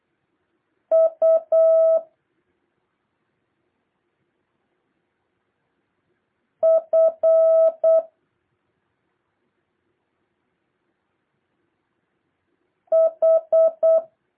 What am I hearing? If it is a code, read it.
Morse code, heard as a keyed tone, about 8 wpm, 650 Hz. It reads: U F H
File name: jh8cbh / 2023cw